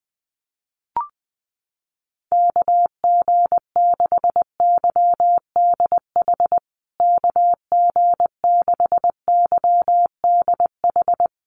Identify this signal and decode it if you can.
Morse code, heard as a keyed tone, about 20 wpm, 700 Hz. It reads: KG6YDH KG6YDH
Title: USA Call Signs Repeated 3x 20wpm